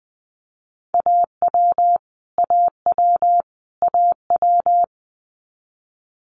Morse code, heard as a keyed tone, about 20 wpm, 700 Hz. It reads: AW AW AW
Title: Multi-letter Phonemes Repeated 3x 20wpm